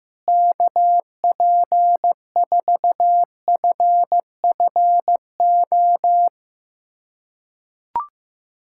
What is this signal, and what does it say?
Morse code, heard as a keyed tone, about 15 wpm, 700 Hz. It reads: KP4FFO
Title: USA Callsign DE Callsign - 6 Characters - 15wpm